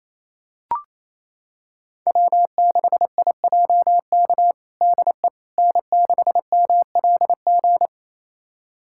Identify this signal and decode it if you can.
Morse code, heard as a keyed tone, about 27 wpm, 700 Hz. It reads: W6IJK DE N6MLG
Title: USA Callsign DE Callsign - 5 Characters - 28wpm